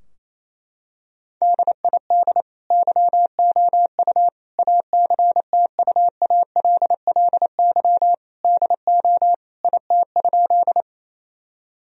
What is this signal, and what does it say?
Morse code, heard as a keyed tone, about 28 wpm, 700 Hz. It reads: DID YOU ACTUALLY DO IT?
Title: Sentences from Top 1000 Words - Review Part 4 - 28wpm